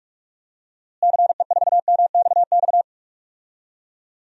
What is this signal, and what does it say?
Morse code, heard as a keyed tone, about 46 wpm, 700 Hz. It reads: CE4MXX
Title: DXCC-100 Call Signs - 6 Characters - Repeated 2x - 45wpm